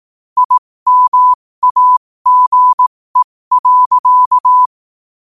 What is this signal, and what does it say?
Morse code, heard as a keyed tone, about 18 wpm, spaced as for 15 wpm, 1000 Hz. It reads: IMAGE.
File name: qotdfiles / 15wpm